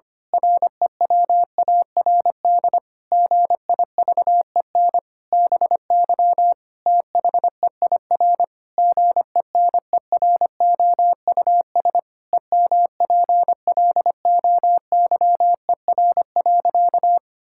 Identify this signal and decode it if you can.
Morse code, heard as a keyed tone, about 25 wpm, 700 Hz. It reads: REWARD GIVEN BY THEIR GENEROUS EMPLOYER.
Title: Pangram Challenge v2 - 25wpm